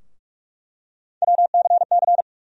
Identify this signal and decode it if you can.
Morse code, heard as a keyed tone, about 43 wpm, 700 Hz. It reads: WCC
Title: USA Call Sign Postfixes - 3 Letters 45wpm